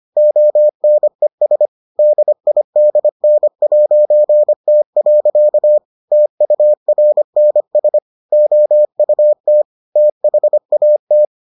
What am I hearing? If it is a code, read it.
Morse code, heard as a keyed tone, about 25 wpm, 600 Hz. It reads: ONES DIDN'T. TURNS OUT THAT